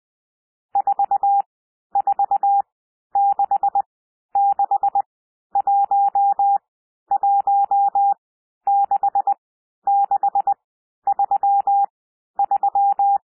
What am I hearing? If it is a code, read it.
Morse code, heard as a keyed tone, about 20 wpm, spaced as for 11 wpm, 800 Hz. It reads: 4466116633
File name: SOUND NUMBERS 20 wpm